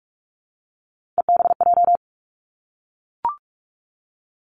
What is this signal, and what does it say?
Morse code, heard as a keyed tone, about 46 wpm, 700 Hz. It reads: EBJ
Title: Three Challenging Letter-Number Combinations 45wpm